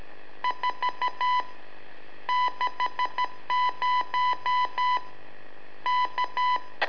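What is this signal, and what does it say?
Morse code, heard as a keyed tone, about 13 wpm, 1000 Hz. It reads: 4 60 K
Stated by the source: Telemetry in Morse from LU7AA and LU4EZ, AMSAT and Turma Mar del Plata, sent from a balloon.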